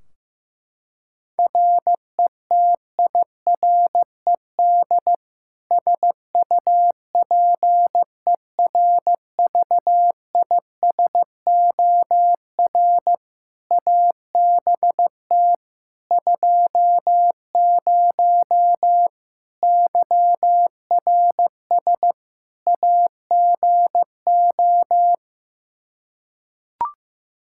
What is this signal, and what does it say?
Morse code, heard as a keyed tone, about 15 wpm, 700 Hz. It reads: RETIRED SUPERVISOR ABT 20 YRS AGO
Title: Practice QSO - Exchange 3 - More Info - 15wpm